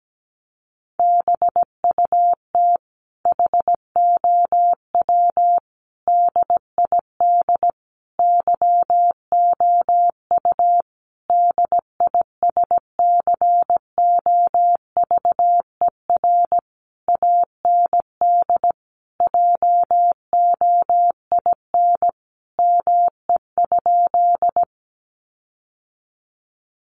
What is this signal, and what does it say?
Morse code, heard as a keyed tone, about 17 wpm, 700 Hz. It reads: BUT HOW DID YOU DISCOVER AND JOIN ME?
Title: Sentences from Top 1400 Words 17wpm